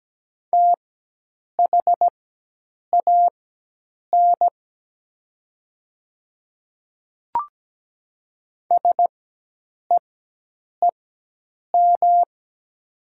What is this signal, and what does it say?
Morse code, heard as a keyed tone, about 17 wpm, spaced as for 8 wpm, 700 Hz. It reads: THAN SEEM
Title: 4x Character Spacing - Top 500 Words - 17wpm